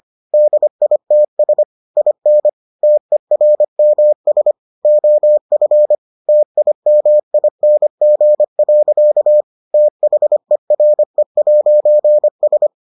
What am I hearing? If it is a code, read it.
Morse code, heard as a keyed tone, about 25 wpm, 600 Hz. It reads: DITS IN TERMS OF TIMING. THERE'S